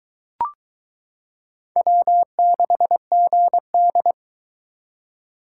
Morse code, heard as a keyed tone, about 22 wpm, 700 Hz. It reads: W6GD